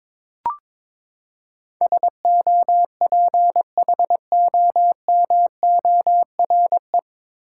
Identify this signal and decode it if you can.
Morse code, heard as a keyed tone, about 22 wpm, 700 Hz. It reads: SOPHOMORE
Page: Morse Code Ninja practice 22wpm